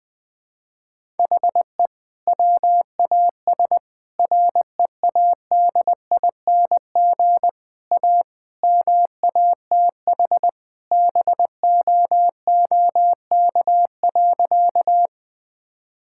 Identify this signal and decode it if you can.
Morse code, heard as a keyed tone, about 20 wpm, 700 Hz. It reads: HE WAS READING A MATH BOOK.
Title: Sentences from Top 1000 Words - Review Part 6 - 20wpm